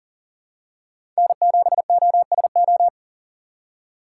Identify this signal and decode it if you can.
Morse code, heard as a keyed tone, about 40 wpm, 700 Hz. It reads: N7OSO
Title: USA Call Signs - 5 Characters - 2x Character Spacing - 40wpm